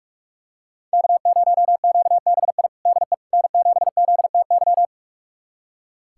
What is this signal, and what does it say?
Morse code, heard as a keyed tone, about 43 wpm, 700 Hz. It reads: K0QBI DE N7ZTY
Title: USA Callsign DE Callsign - 5 Characters - 45wpm